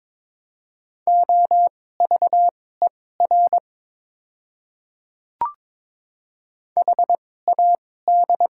Morse code, heard as a keyed tone, about 21 wpm, spaced as for 16 wpm, 700 Hz. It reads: OVER HAD
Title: 2x Character Spacing - Top 100 Words - 22wpm